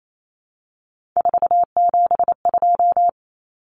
Morse code, heard as a keyed tone, about 28 wpm, 700 Hz. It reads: 472
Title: Numbers - 3-Digit 28wpm